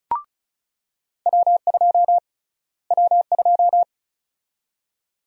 Morse code, heard as a keyed tone, about 36 wpm, 700 Hz. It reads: W2 W2
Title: USA Call Sign Prefixes - Repeated 2x - 35wpm